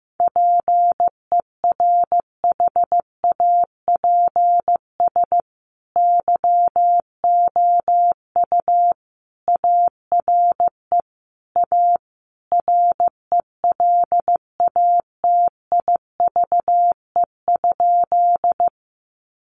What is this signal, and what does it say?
Morse code, heard as a keyed tone, about 15 wpm, 700 Hz. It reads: PERHAPS YOU ARE A RELATIVE?
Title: Sentences from Top 1400 Words 15wpm